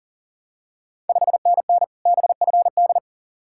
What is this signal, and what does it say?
Morse code, heard as a keyed tone, about 40 wpm, 700 Hz. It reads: 5NN BFD
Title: Ontario QSO Party Exchange 40wpm